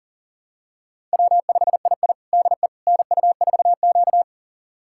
Morse code, heard as a keyed tone, about 40 wpm, 700 Hz. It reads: W5II DE NU4Q
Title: USA Callsign DE Callsign - 4 Characters - 40wpm